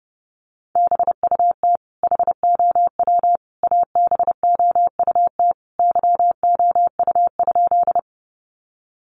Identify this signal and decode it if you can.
Morse code, heard as a keyed tone, about 30 wpm, 700 Hz. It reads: BUT HOW ABOUT YOU?
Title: Sentences from Top 100 Words 30wpm